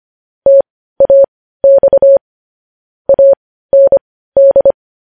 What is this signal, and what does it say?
Morse code, heard as a keyed tone, about 25 wpm, spaced as for 15 wpm, 550 Hz. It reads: TAX AND
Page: news headlines in morse code at 15 wpm